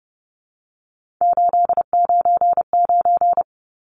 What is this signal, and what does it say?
Morse code, heard as a keyed tone, about 30 wpm, 700 Hz. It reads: 899
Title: Numbers - 3-Digit 30wpm